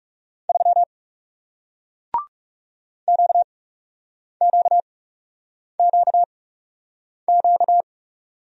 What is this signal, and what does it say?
Morse code, heard as a keyed tone, about 40 wpm, spaced as for 8 wpm, 700 Hz. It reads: 3 QQQQ